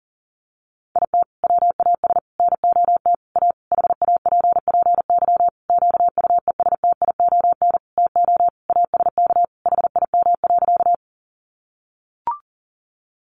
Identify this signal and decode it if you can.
Morse code, heard as a keyed tone, about 40 wpm, 700 Hz. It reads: IT WAS NOT A HAPPY QUESTION TO ASK HIM.
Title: Sentences from Top 1000 Words 40wpm